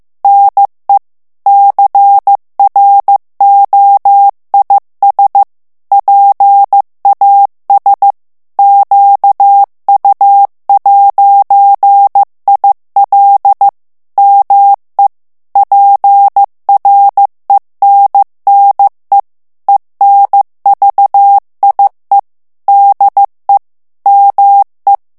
Morse code, heard as a keyed tone, about 15 wpm, 800 Hz. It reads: NE CROIS PAS QU'IL ME PRENNE ENVIE DE ME
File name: FRANCAIS TEXTE CLAIR 2 VITESSE 14 MOTS MINUTES